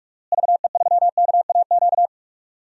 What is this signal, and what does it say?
Morse code, heard as a keyed tone, about 41 wpm, 700 Hz. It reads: VE3KAQ
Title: DXCC-100 Call Signs - 6 Characters - Repeated 2x - 45wpm